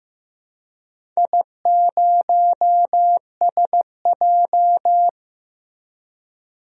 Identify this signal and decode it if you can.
Morse code, heard as a keyed tone, about 15 wpm, 700 Hz. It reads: I0SJ
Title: DXCC-100 Call Signs - 4 Characters - 15wpm